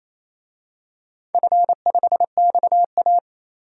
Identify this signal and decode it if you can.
Morse code, heard as a keyed tone, about 27 wpm, 700 Hz. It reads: F5XA
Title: DXCC-100 Call Signs - 4 Characters - 28wpm